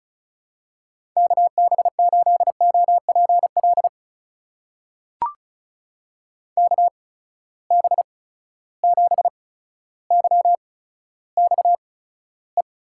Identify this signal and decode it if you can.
Morse code, heard as a keyed tone, about 36 wpm, spaced as for 15 wpm, 700 Hz. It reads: KB8OPL KB7YXE
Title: USA Call Signs - 8x Character Spacing - 35wpm